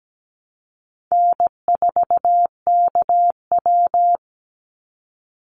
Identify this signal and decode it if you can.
Morse code, heard as a keyed tone, about 17 wpm, 700 Hz. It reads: N4KW
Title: Call Signs Repeated 3x - 17wpm